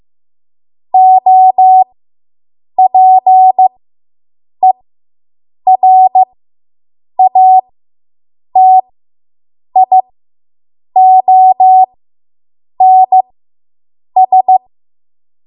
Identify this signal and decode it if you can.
Morse code, heard as a keyed tone, about 15 wpm, spaced as for 7 wpm, 750 Hz. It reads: OPERATIONS